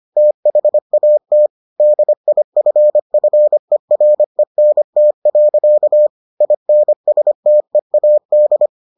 Morse code, heard as a keyed tone, about 25 wpm, 600 Hz. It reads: THAT DIFFERENT. INSTEAD